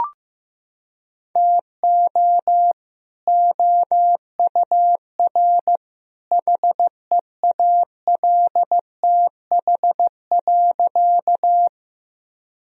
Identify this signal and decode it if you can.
Morse code, heard as a keyed tone, about 15 wpm, 700 Hz. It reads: TO OUR HEALTH.
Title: Sentences from Top 1000 Words - Review Part 3 - 15wpm